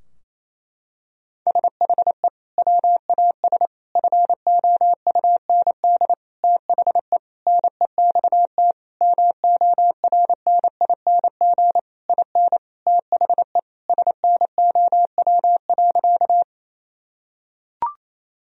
Morse code, heard as a keyed tone, about 27 wpm, 700 Hz. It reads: SHE WAS FOUND THE NEXT MORNING IN THE SNOW.
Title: Sentences from Top 1000 Words - Review Part 3 - 28wpm